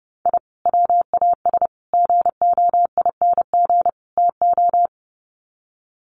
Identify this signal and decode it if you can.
Morse code, heard as a keyed tone, about 30 wpm, 700 Hz. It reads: I WAS GOING TO
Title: Sets of 4 Words 30wpm